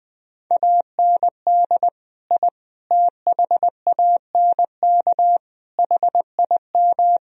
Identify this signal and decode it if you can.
Morse code, heard as a keyed tone, about 20 wpm, 700 Hz. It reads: AND I THANK HIM
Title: Sentences from Top 1000 Words - Review Part 3 - 20wpm